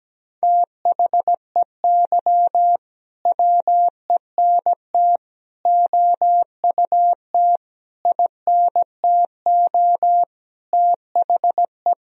Morse code, heard as a keyed tone, about 17 wpm, 700 Hz. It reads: THEY WENT OUT INTO THE
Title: Sentences from Top 1000 Words - Review Part 3 - 17wpm